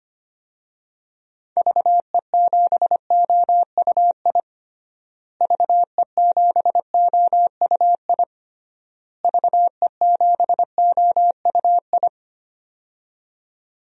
Morse code, heard as a keyed tone, about 25 wpm, 700 Hz. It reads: VE7OUI VE7OUI VE7OUI
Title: DXCC-100 Call Signs - 6 Characters - Repeated 3x - 25wpm